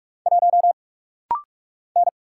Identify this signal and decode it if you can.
Morse code, heard as a keyed tone, about 44 wpm, 700 Hz. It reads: 1 N
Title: Single Letter-Number Alt - Rapid-Fire 45wpm